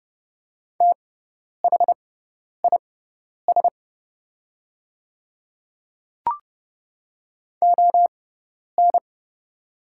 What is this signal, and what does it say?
Morse code, heard as a keyed tone, about 30 wpm, spaced as for 10 wpm, 690 Hz. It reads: THIS ON
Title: 6x Character Spacing - Top 100 Words - 30wpm